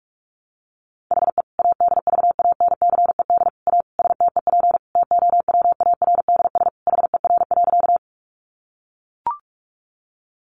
Morse code, heard as a keyed tone, about 44 wpm, 700 Hz. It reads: HE ADVANCED A STEP TOWARDS HER.
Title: Sentences from Top 1200 Words 45wpm